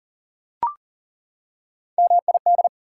Morse code, heard as a keyed tone, about 37 wpm, 700 Hz. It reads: MID
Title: Prefixes and Suffixes 40wpm